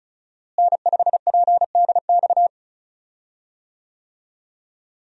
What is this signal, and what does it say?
Morse code, heard as a keyed tone, about 36 wpm, 700 Hz. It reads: N5PDX
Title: USA Call Signs - 5 Characters - Repeated 3x - 35wpm